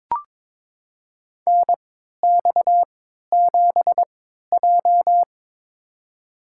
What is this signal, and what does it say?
Morse code, heard as a keyed tone, about 21 wpm, spaced as for 12 wpm, 700 Hz. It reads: NX7J
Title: USA Call Signs - 4 Characters - 3x Character Spacing - 22wpm